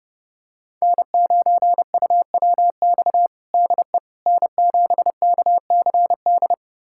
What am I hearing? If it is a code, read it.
Morse code, heard as a keyed tone, about 30 wpm, 700 Hz. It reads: N9UWX DE N7KCD
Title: USA Callsign DE Callsign - 5 Characters - 30wpm